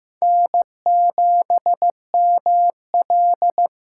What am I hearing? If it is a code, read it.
Morse code, heard as a keyed tone, about 15 wpm, 700 Hz. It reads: N7ML
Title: K1USN SST QSO Practice - 15wpm